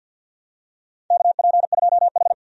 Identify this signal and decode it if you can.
Morse code, heard as a keyed tone, about 52 wpm, 700 Hz. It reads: KP2H
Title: DXCC-100 Call Signs - 4 Characters - 50wpm